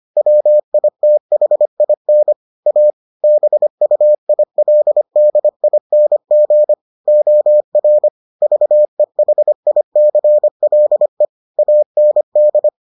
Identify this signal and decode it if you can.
Morse code, heard as a keyed tone, about 25 wpm, 600 Hz. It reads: WITHIN A BUILDING OR VEHICLE AND